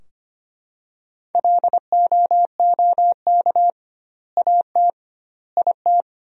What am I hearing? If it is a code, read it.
Morse code, heard as a keyed tone, about 25 wpm, 700 Hz. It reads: LOOK AT IT
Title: 2x Word-Spacing - Sentences from Top 100 Words - 25wpm